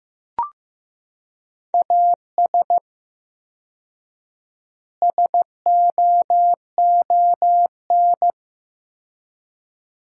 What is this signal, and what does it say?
Morse code, heard as a keyed tone, about 15 wpm, 700 Hz. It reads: AS SOON
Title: 4x Word-Spacing - Sentences from Top 400 Words - 15wpm